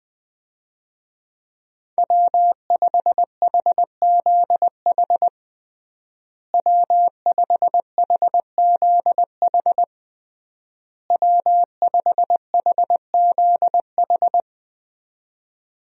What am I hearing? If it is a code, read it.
Morse code, heard as a keyed tone, about 20 wpm, 700 Hz. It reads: W5HZH W5HZH W5HZH
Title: USA Call Signs - 5 Characters - Repeated 3x - 20wpm